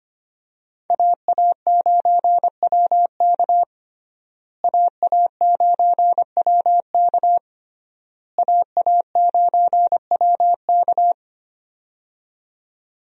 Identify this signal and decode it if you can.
Morse code, heard as a keyed tone, about 25 wpm, 700 Hz. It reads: AA9WK AA9WK AA9WK
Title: USA Call Signs - 5 Characters - Repeated 3x - 25wpm